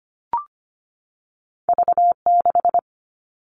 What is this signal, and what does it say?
Morse code, heard as a keyed tone, about 25 wpm, 700 Hz. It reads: V6
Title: All Cut Numbers - 2-Digit 25wpm